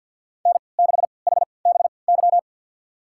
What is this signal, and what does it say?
Morse code, heard as a keyed tone, about 51 wpm, spaced as for 28 wpm, 700 Hz. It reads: N6HBY
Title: USA Call Signs - 5 Characters - 3x Character Spacing - 50wpm